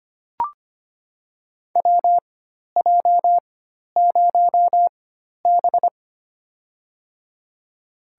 Morse code, heard as a keyed tone, about 25 wpm, spaced as for 12 wpm, 700 Hz. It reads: WJ0B
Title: USA Call Signs - 4x Character Spacing - 25wpm